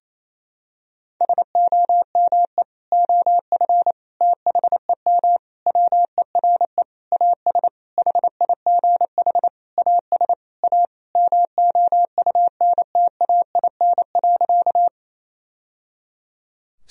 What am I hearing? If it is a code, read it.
Morse code, heard as a keyed tone, about 28 wpm, 700 Hz. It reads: SOME OF THEM WERE AS HIGH AS A MOUNTAIN.
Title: Sentences from Top 1000 Words - Review Part 1 - 28wpm